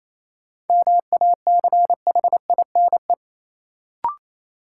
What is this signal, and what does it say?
Morse code, heard as a keyed tone, about 28 wpm, 700 Hz. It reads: MACHINE
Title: Sentences from Top 1000 Words - Review Part 2 - New Word Drill - 28wpm